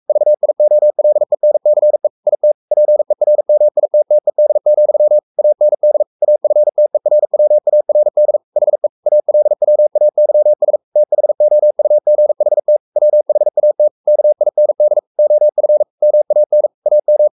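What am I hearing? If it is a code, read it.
Morse code, heard as a keyed tone, about 44 wpm, 600 Hz. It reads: VIOLENCE IT PERMITTED, AND AFTERWARD HE ALWAYS THOUGHT WHAT KIND OF MAN AM
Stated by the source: Morse code at 43 wpm sending Ken Follett's novel Never in English, from CW med Gnister.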